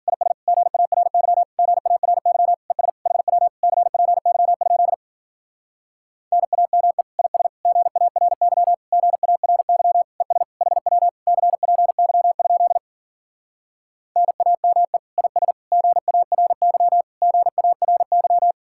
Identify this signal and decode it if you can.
Morse code, heard as a keyed tone, about 52 wpm, 700 Hz. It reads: IS GARY GARY ES HW CPY? NAME IS GARY GARY ES HW CPY? NAME IS GARY GARY